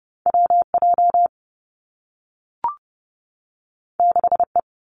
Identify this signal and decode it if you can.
Morse code, heard as a keyed tone, about 30 wpm, 700 Hz. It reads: W J 6 E
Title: Two Challenging Letter-Number Combinations 30wpm